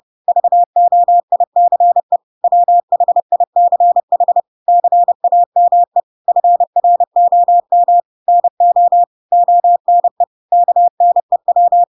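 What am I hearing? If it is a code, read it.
Morse code, heard as a keyed tone, about 30 wpm, 700 Hz. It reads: VOICE WHICH CAME FROM NO ONE KNEW
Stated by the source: Chapter 2 of The Adventures of Pinocchio read as a book in Morse code (CW) at 30 wpm, by ik1hge.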